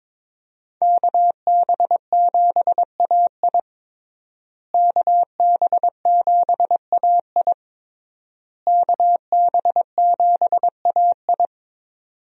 Morse code, heard as a keyed tone, about 22 wpm, 700 Hz. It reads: KB7AI KB7AI KB7AI
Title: USA Call Signs - 5 Characters - Repeated 3x - 22wpm